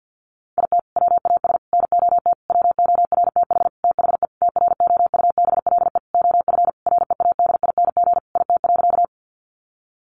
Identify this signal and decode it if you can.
Morse code, heard as a keyed tone, about 50 wpm, 700 Hz. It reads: IT WAS NOT WORTH THE TROUBLE OF READING IT.
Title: Sentences from Top 1200 Words 50wpm